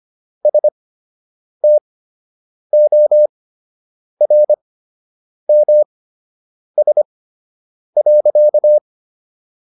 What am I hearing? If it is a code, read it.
Morse code, heard as a keyed tone, about 25 wpm, spaced as for 8 wpm, 600 Hz. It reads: STORMS.